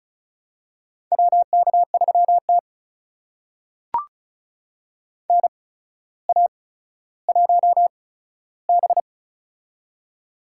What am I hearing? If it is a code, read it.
Morse code, heard as a keyed tone, about 36 wpm, spaced as for 15 wpm, 700 Hz. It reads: WK3T NA1B
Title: USA Call Signs - 4 Characters - 8x Character Spacing - 35wpm